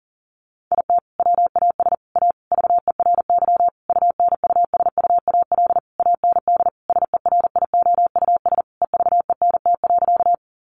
Morse code, heard as a keyed tone, about 40 wpm, 700 Hz. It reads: IT WAS A VERY UNUSUAL AND SERIOUS EVENT.
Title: Sentences from Top 1400 Words 40wpm